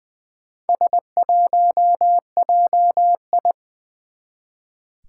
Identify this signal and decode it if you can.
Morse code, heard as a keyed tone, about 20 wpm, 700 Hz. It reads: S1JI
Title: Four Challenging Letter-Number Combinations 20wpm